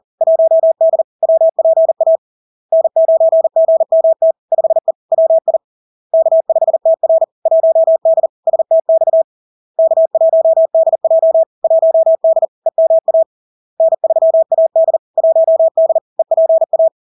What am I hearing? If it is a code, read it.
Morse code, heard as a keyed tone, about 40 wpm, 650 Hz. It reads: 1D WPA N9GMT 5E WI K5TR 1D STX K1DJ 1D EMA N3AD 1D EPA
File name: LARK-FD-2022-Log-in-CW-40wpm-650-Hz